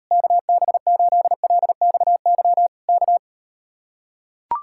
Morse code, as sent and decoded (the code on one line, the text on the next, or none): -.- -... ---.. .-.. -..- -.-- / -.-
KB8LXY K